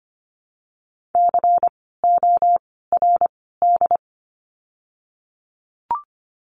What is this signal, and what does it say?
Morse code, heard as a keyed tone, about 25 wpm, 700 Hz. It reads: C O R D
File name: course / 25:10wpm / 094 Lesson 16f - Letter c - Words 25wpm